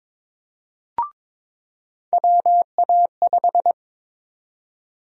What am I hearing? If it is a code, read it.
Morse code, heard as a keyed tone, about 22 wpm, 700 Hz. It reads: WA5
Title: USA Call Sign Prefixes - 22wpm